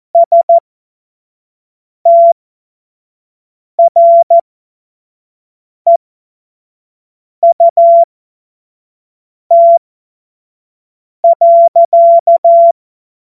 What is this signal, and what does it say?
Morse code, heard as a keyed tone, about 14 wpm, spaced as for 5 wpm, 680 Hz. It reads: STREUT.
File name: Morsekurs / Witze / Witz 07. September 2021, 25 F70